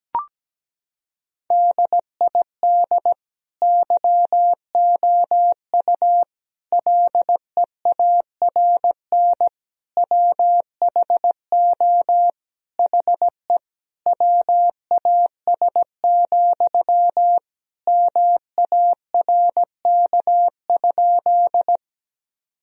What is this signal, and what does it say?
Morse code, heard as a keyed tone, about 17 wpm, 700 Hz. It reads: DID YOU LEARN WHO HE WAS, MARK?
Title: Sentences from Top 1000 Words - Review Part 1 - 17wpm